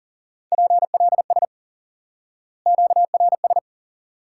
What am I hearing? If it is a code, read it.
Morse code, heard as a keyed tone, about 40 wpm, 700 Hz. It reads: PLS QRS
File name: IQE Course 3 - Lesson 89 - YAGI - Two Elements - 40wpm